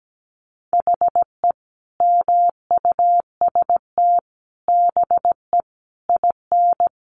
Morse code, heard as a keyed tone, about 17 wpm, 700 Hz. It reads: HE MUST BE IN
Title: Sentences from Top 500 Words 17wpm